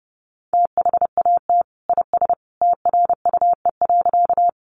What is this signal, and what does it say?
Morse code, heard as a keyed tone, about 28 wpm, 700 Hz. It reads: THAT IS TRUE.
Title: Sentences from Top 400 Words 30wpm